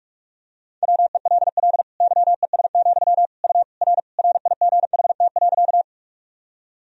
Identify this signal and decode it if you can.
Morse code, heard as a keyed tone, about 46 wpm, 700 Hz. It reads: WELL YES, U R RIGHT.
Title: Sentences from Top 500 Words and Abbreviations 45wpm